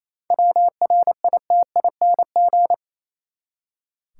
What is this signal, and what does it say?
Morse code, heard as a keyed tone, about 28 wpm, 700 Hz. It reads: WRITING
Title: Seven Letter Words - 28wpm